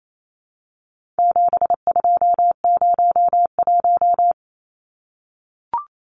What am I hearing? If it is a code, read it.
Morse code, heard as a keyed tone, about 28 wpm, 700 Hz. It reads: 7201
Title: Numbers - 4-Digit 28wpm